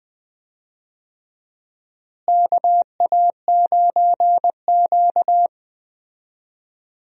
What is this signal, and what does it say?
Morse code, heard as a keyed tone, about 20 wpm, 700 Hz. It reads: KA9Q
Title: USA Call Signs - 4 Characters - Repeated 2x - 20wpm